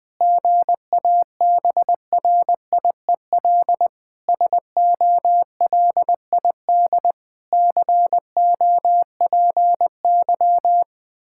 Morse code, heard as a keyed tone, about 20 wpm, 700 Hz. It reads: GABRIEL SOLID COPY
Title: Practice QSO - Exchange 3 - More Info - 20wpm